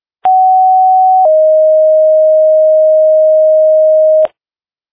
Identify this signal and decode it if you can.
Two-tone page, heard as a one-second tone, then a three-second tone, about 750 then 620 Hz. It is heard from 0.2 to 4.3 s.